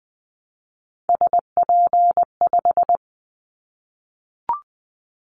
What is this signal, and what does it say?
Morse code, heard as a keyed tone, about 20 wpm, 700 Hz. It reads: SP5
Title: Callsign Prefixes for Top 25 DXCC - 20wpm